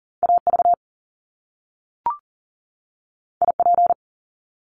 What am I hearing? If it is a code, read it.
Morse code, heard as a keyed tone, about 42 wpm, 700 Hz. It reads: AV IP
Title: Top Two Letter Combinations in Words 40wpm